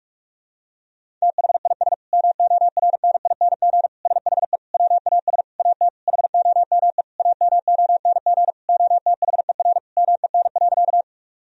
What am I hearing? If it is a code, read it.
Morse code, heard as a keyed tone, about 46 wpm, 700 Hz. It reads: THIS MORNING SHE WAS AT HOME AMONG OTHER MEN.
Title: Sentences from Top 1000 Words - Review Part 3 - 45wpm